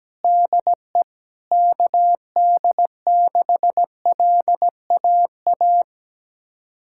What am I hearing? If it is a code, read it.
Morse code, heard as a keyed tone, about 17 wpm, 700 Hz. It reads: DE KD6LAA
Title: USA Callsign DE Callsign - 6 Characters - 17wpm